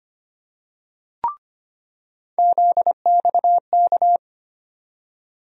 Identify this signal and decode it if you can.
Morse code, heard as a keyed tone, about 25 wpm, 700 Hz. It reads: ZXK
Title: USA Call Sign Postfixes - 3 Letters 25wpm